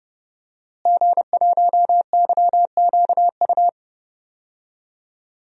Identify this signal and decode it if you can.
Morse code, heard as a keyed tone, about 30 wpm, 700 Hz. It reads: G1YQU
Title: DXCC-100 Call Signs - 5 Characters - Repeated 2x - 30wpm